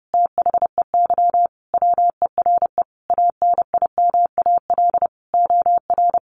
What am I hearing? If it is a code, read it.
Morse code, heard as a keyed tone, about 30 wpm, 700 Hz. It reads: THEY WERE ANIMAL OR
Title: Sentences from Top 1300 Words 30wpm